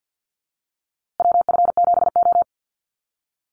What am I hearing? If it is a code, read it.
Morse code, heard as a keyed tone, about 50 wpm, 700 Hz. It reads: WF7O
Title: USA Call Signs Repeated 2x 50wpm